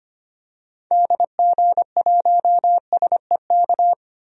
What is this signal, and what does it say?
Morse code, heard as a keyed tone, about 25 wpm, 700 Hz. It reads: DG1SEK
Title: DXCC-100 Call Signs - 6 Characters - 25wpm